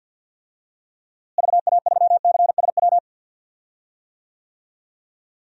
Morse code, heard as a keyed tone, about 50 wpm, 700 Hz. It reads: VA3CSW
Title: DXCC-100 Call Signs - 6 Characters - Repeated 2x - 50wpm